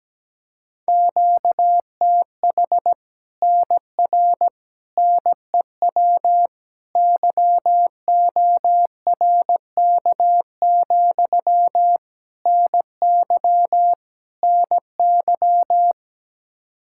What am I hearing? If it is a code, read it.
Morse code, heard as a keyed tone, about 17 wpm, 700 Hz. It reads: QTH NR NEW YORK, NY NY
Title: Practice QSO - Exchange 1 - Beginning - 17wpm